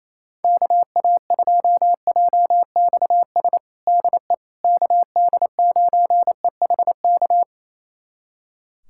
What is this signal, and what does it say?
Morse code, heard as a keyed tone, about 28 wpm, 700 Hz. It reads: KA2JXS DE KD9EHK
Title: USA Callsign DE Callsign - 6 Characters - 28wpm